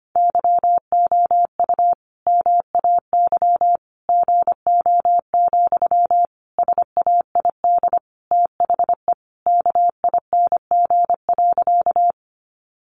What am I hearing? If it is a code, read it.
Morse code, heard as a keyed tone, about 25 wpm, 700 Hz. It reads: YOU MAY GO, SAID THE KING.
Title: Sentences from Top 400 Words 25wpm